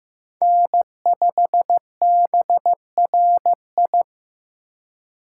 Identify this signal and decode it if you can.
Morse code, heard as a keyed tone, about 18 wpm, 700 Hz. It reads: N5BRI
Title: Speed Racing - USA 1x3 Call Signs 15wpm